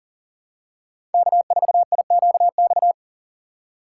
Morse code, heard as a keyed tone, about 40 wpm, 700 Hz. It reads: K4IQX
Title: USA Call Signs - 5 Characters - 6x Character Spacing - 40wpm